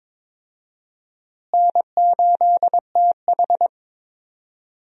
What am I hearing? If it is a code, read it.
Morse code, heard as a keyed tone, about 22 wpm, 700 Hz. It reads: N8TH
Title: USA Call Signs - 4 Characters - Repeated 3x - 22wpm